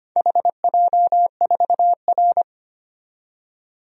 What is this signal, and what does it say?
Morse code, heard as a keyed tone, about 25 wpm, 700 Hz. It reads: HJ4R
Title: DXCC-100 Call Signs - 4 Characters - 25wpm